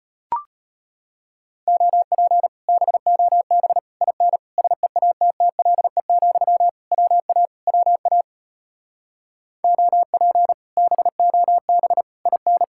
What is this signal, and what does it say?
Morse code, heard as a keyed tone, about 33 wpm, 700 Hz. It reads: OP BOB IN SEATTLE, WA WA OP BOB IN